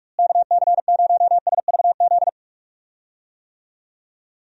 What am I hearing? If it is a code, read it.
Morse code, heard as a keyed tone, about 41 wpm, 700 Hz. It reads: KC0SVZ
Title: USA Call Signs - 6 Characters - Repeated 3x - 45wpm